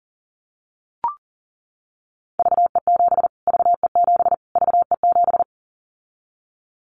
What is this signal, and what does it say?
Morse code, heard as a keyed tone, about 40 wpm, 700 Hz. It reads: VE7 VE7 VE7
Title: Callsign Prefixes for Top 25 DXCC - Repeated 3x - 40wpm